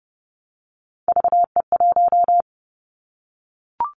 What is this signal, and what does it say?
Morse code, heard as a keyed tone, about 30 wpm, 700 Hz. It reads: VE1
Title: Callsign Prefixes for Top 100 DXCC - 30wpm